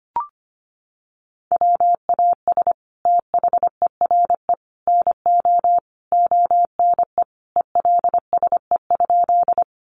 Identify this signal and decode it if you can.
Morse code, heard as a keyed tone, about 25 wpm, 700 Hz. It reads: WAS THERE NO ONE ELSE?
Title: Sentences from Top 800 Words 25wpm